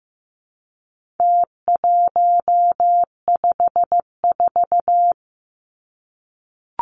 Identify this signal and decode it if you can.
Morse code, heard as a keyed tone, about 15 wpm, 700 Hz. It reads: T154
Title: Numbers - Contest Style - 4-Digit 15wpm